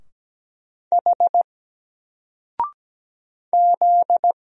Morse code, heard as a keyed tone, about 17 wpm, 700 Hz. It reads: H Z
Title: Single Letter-Number Alt 17wpm